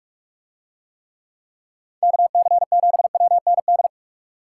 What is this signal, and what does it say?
Morse code, heard as a keyed tone, about 44 wpm, 700 Hz. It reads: KC7WND